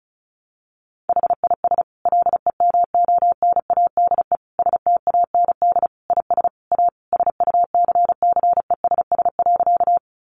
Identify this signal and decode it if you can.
Morse code, heard as a keyed tone, about 34 wpm, 700 Hz. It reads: HIS LEMONADE STAND IS A SUCCESS.
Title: Additional Easy Phrases 35wpm